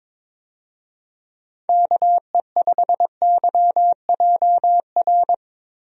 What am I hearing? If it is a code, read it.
Morse code, heard as a keyed tone, about 22 wpm, 700 Hz. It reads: KE5YJR
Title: USA Call Signs - 6 Characters - Repeated 3x - 22wpm